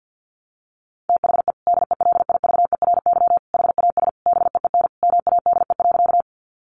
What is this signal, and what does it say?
Morse code, heard as a keyed tone, about 50 wpm, 700 Hz. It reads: THE DELIVERY HAS BEEN MADE.
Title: Sentences from Top 1500 Words 50wpm